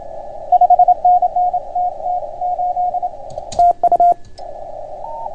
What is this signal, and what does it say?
Morse code, heard as a keyed tone, about 29 wpm, 690 Hz. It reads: 5NN TT8 TU